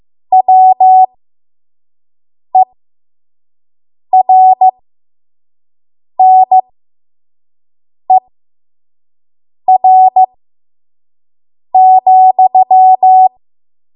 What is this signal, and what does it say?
Morse code, heard as a keyed tone, about 15 wpm, spaced as for 5 wpm, 750 Hz. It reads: WERNER,
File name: Morse / 05WPM